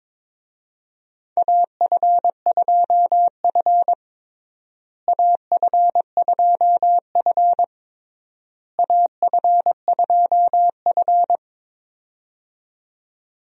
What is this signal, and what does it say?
Morse code, heard as a keyed tone, about 22 wpm, 700 Hz. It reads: AF2F AF2F AF2F
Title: USA Call Signs - 4 Characters - Repeated 3x - 22wpm